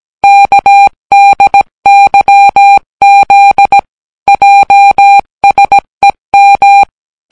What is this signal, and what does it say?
Morse code, heard as a keyed tone, about 17 wpm, 800 Hz. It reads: KDYZ JSEM